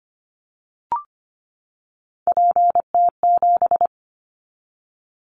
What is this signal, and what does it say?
Morse code, heard as a keyed tone, about 25 wpm, 700 Hz. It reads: PT7
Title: Callsign Prefixes for Top 50 DXCC - 25wpm